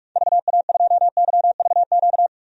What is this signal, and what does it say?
Morse code, heard as a keyed tone, about 46 wpm, 700 Hz. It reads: VA2YVQ
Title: DXCC-100 Call Signs - 6 Characters - 45wpm